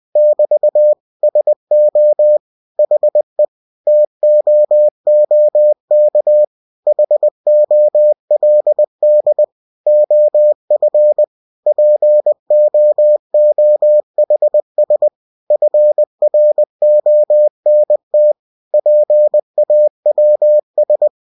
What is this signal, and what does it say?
Morse code, heard as a keyed tone, about 20 wpm, 600 Hz. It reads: <BT> SO HE TOOK HOLD OF POOHS FRONT PAWS
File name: cw-books / winnie-the-pooh / winnie-20wpm-page-0019